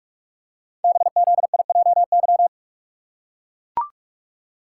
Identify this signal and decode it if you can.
Morse code, heard as a keyed tone, about 44 wpm, 700 Hz. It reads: BZIJY
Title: Code Groups - 5 Letters - 45wpm